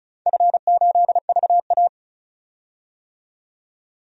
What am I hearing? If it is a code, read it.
Morse code, heard as a keyed tone, about 36 wpm, 700 Hz. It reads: F8VA
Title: DXCC-100 Call Signs - 4 Characters - 35wpm